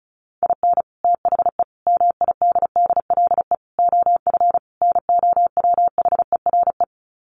Morse code, heard as a keyed tone, about 34 wpm, 700 Hz. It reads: IN THE MIDDLE OF NOWHERE
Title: Sets of 5 Words Encore 35wpm